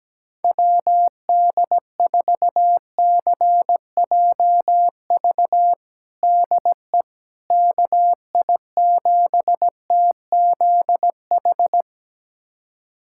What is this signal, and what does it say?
Morse code, heard as a keyed tone, about 17 wpm, 700 Hz. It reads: WD4CJV DE KI7TZH
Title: USA Callsign DE Callsign - 6 Characters - 17wpm